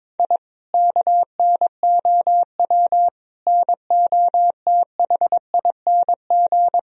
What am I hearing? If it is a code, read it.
Morse code, heard as a keyed tone, about 22 wpm, 700 Hz. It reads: I KNOW NOTHING
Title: Sentences from Top 1000 Words - Review Part 4 - 22wpm